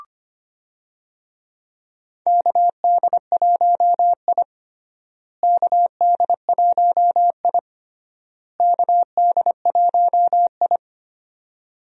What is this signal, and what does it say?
Morse code, heard as a keyed tone, about 25 wpm, 700 Hz. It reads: KD1I KD1I KD1I
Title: USA Call Signs - 4 Characters - Repeated 3x - 25wpm